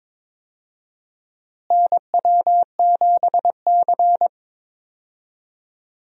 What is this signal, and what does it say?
Morse code, heard as a keyed tone, about 22 wpm, 700 Hz. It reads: NW7C
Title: USA Call Signs - 4 Characters - Repeated 3x - 22wpm